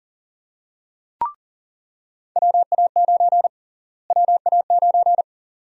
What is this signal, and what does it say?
Morse code, heard as a keyed tone, about 40 wpm, 700 Hz. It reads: WA9 WA9
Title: USA Call Sign Prefixes - Repeated 2x - 40wpm